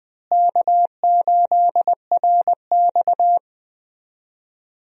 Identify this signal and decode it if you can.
Morse code, heard as a keyed tone, about 20 wpm, 700 Hz. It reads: K8RX
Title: USA Call Signs - 4 Characters - Repeated 2x - 20wpm